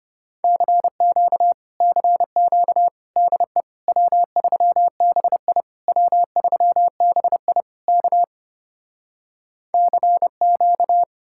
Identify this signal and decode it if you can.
Morse code, heard as a keyed tone, about 33 wpm, 700 Hz. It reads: CQ CQ DE W3BI W3BI K CQ